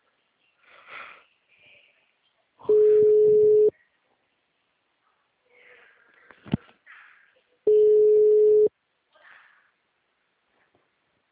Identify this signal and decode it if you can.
Ringback tone (Europe, 425 Hz).
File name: out- -2039- - - .16808